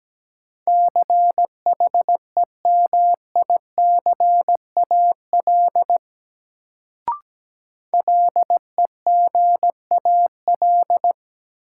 Morse code, heard as a keyed tone, about 17 wpm, 700 Hz. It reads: CHEMICAL LEGAL
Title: Sentences from Top 1200 Words - New Word Drill - 17wpm